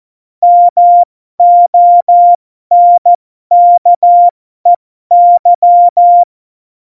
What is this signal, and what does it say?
Morse code, heard as a keyed tone, about 14 wpm, spaced as for 12 wpm, 700 Hz. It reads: MONKEY
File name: Morsekurs / Witze